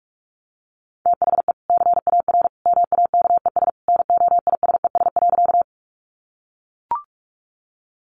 Morse code, heard as a keyed tone, about 44 wpm, 700 Hz. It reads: THE CAR MAKES NOISES.
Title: Easy Phrases 45wpm